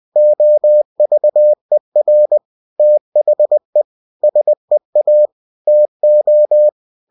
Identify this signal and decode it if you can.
Morse code, heard as a keyed tone, about 20 wpm, 600 Hz. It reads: OVER THE SEA TO
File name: cw-books / winnie-the-pooh / winnie-20wpm-page-0101